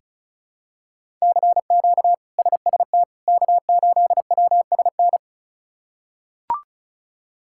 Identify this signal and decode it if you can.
Morse code, heard as a keyed tone, about 34 wpm, 700 Hz. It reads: CQ SST K8WSN
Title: K1USN SST QSO Practice - 35wpm